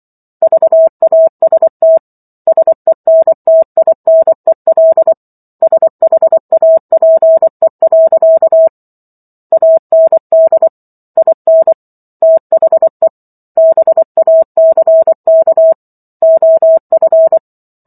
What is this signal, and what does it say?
Morse code, heard as a keyed tone, about 24 wpm, 650 Hz. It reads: VAST SENTINEL SHAPE. AND IN THE BACK OF
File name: Chapter 15 - 24 WPM